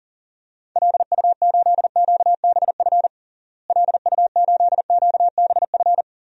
Morse code, heard as a keyed tone, about 40 wpm, 700 Hz. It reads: LU8QBF LU8QBF
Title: DXCC-100 Call Signs - 6 Characters - Repeated 2x - 40wpm